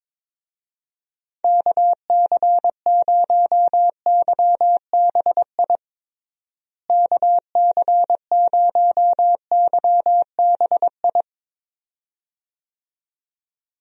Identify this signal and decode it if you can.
Morse code, heard as a keyed tone, about 22 wpm, 700 Hz. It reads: KC0YBI KC0YBI
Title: USA Call Signs - 6 Characters - Repeated 2x - 22wpm